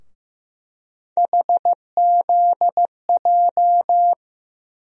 Morse code, heard as a keyed tone, about 15 wpm, 700 Hz. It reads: HZJ
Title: USA Call Sign Postfixes - 3 Letters 15wpm